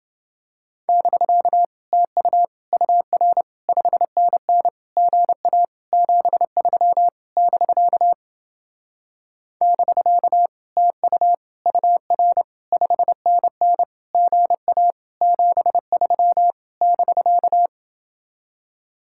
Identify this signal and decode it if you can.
Morse code, heard as a keyed tone, about 29 wpm, 700 Hz. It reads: <BK> TU UR 5NN GA 73 <BK> <BK> TU UR 5NN GA 73 <BK>